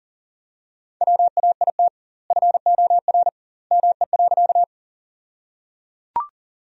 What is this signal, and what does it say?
Morse code, heard as a keyed tone, about 38 wpm, 700 Hz. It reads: WAIT FOR ME.
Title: 2x Word-Spacing - Sentences from Top 500 Words - 40wpm